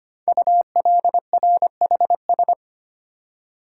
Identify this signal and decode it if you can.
Morse code, heard as a keyed tone, about 25 wpm, 700 Hz. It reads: ULRHS
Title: Code Groups - 5 Letters - 25wpm